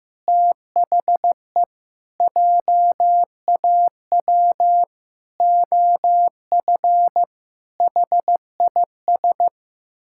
Morse code, heard as a keyed tone, about 15 wpm, 700 Hz. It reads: THE JAW OF HIS
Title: Pangram Challenge v2 - 15wpm